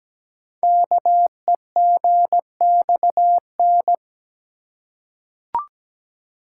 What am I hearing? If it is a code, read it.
Morse code, heard as a keyed tone, about 17 wpm, 700 Hz. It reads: KEGXN
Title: Code Groups - 5 Letters - 17wpm